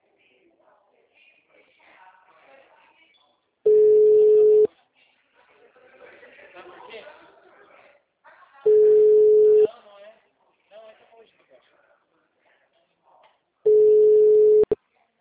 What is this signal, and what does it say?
Ringback tone (Europe, 425 Hz), with speech.